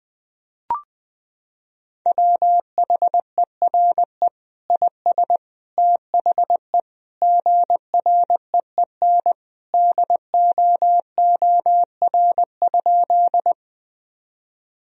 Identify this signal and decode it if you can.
Morse code, heard as a keyed tone, about 20 wpm, 700 Hz. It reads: WHERE IS THE GREEN DOOR?
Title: Sentences from Top 1000 Words - Review Part 2 - 20wpm